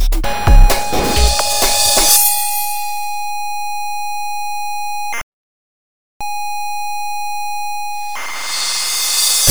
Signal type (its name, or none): computer data on cassette